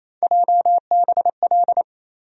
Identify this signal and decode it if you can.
Morse code, heard as a keyed tone, about 28 wpm, 700 Hz. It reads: JBL
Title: USA Call Sign Postfixes - 3 Letters 28wpm